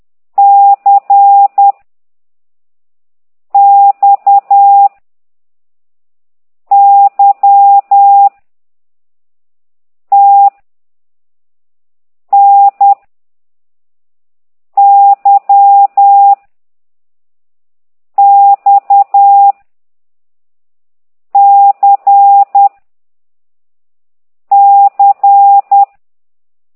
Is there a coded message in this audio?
Yes: Morse code from 0.4 to 25.8 s, heard as a keyed tone, about 10 wpm, 800 Hz. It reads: C X Y T N Y X C C